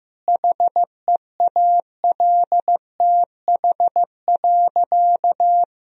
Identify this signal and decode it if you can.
Morse code, heard as a keyed tone, about 15 wpm, 700 Hz. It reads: HEALTH.
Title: Sentences from Top 1000 Words - Review Part 3 - 15wpm